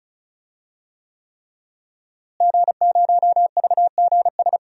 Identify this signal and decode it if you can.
Morse code, heard as a keyed tone, about 34 wpm, 700 Hz. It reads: G0VGS
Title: DXCC-100 Call Signs - 5 Characters - Repeated 3x - 35wpm